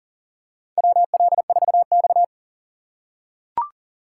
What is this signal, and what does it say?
Morse code, heard as a keyed tone, about 37 wpm, 700 Hz. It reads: WL4X
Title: USA Call Signs - 4 Characters - 4x Character Spacing - 40wpm